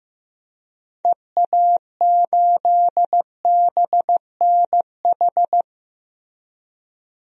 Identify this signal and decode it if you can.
Morse code, heard as a keyed tone, about 16 wpm, 700 Hz. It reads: EA8BNH
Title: DXCC-100 Call Signs - 6 Characters - Repeated 3x - 15wpm